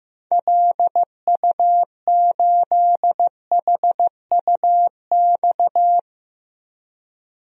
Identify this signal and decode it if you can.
Morse code, heard as a keyed tone, about 15 wpm, 700 Hz. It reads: LU8HUX
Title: DXCC-100 Call Signs - 6 Characters - 15wpm